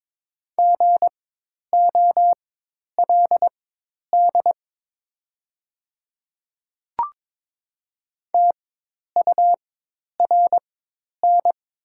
Morse code, heard as a keyed tone, about 22 wpm, spaced as for 10 wpm, 700 Hz. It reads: GOLD TURN
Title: 4x Character Spacing - Top 500 Words - 22wpm